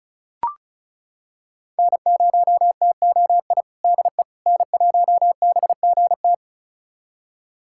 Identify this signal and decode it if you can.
Morse code, heard as a keyed tone, about 36 wpm, 700 Hz. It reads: N0TOI DE N1BGT
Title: USA Callsign DE Callsign - 5 Characters - 35wpm